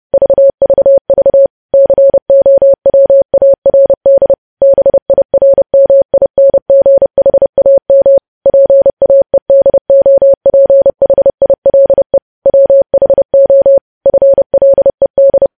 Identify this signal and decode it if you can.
Morse code, heard as a keyed tone, about 30 wpm, 550 Hz. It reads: VVV COWARD BIRMINGHAM PAEDOPHILE WHO FLED
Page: news headlines in morse code at 30 wpm